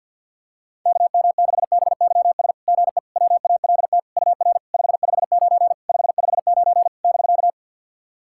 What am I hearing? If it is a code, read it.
Morse code, heard as a keyed tone, about 48 wpm, 690 Hz. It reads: KM6BYS GE WALT UR 559 559 <BK>
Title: SOTA QSO Practice 50wpm